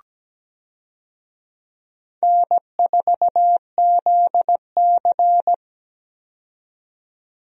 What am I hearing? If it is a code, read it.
Morse code, heard as a keyed tone, about 17 wpm, 700 Hz. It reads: N4ZC
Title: USA Call Signs - 4 Characters - 17wpm